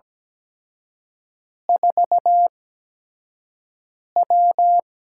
Morse code, heard as a keyed tone, about 17 wpm, spaced as for 5 wpm, 700 Hz. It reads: 4W
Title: USA Call Signs - 5 Characters - 8x Character Spacing - 17wpm